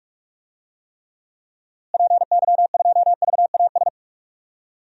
Morse code, heard as a keyed tone, about 46 wpm, 700 Hz. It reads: PY2VAS
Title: DXCC-100 Call Signs - 6 Characters - Repeated 2x - 45wpm